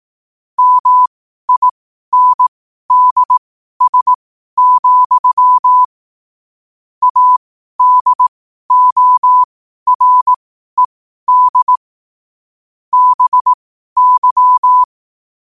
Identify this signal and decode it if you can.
Morse code, heard as a keyed tone, about 18 wpm, spaced as for 12 wpm, 1000 Hz. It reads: MINDS, ADORED BY
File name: qotdfiles / 12wpm